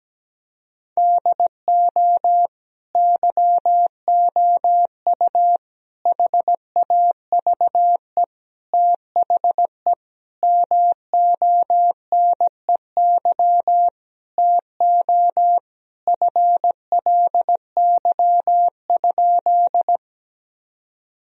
Morse code, heard as a keyed tone, about 17 wpm, 700 Hz. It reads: DO YOU HAVE THE MONEY TO FLY?
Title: Sentences from Top 1000 Words - Review Part 2 - 17wpm